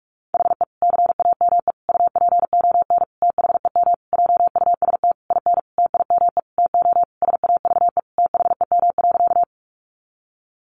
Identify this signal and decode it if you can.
Morse code, heard as a keyed tone, about 44 wpm, 700 Hz. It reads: HE CAME UPON THEM JUST IN TIME TO SAVE THEM.
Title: Sentences from Top 1100 Words 45wpm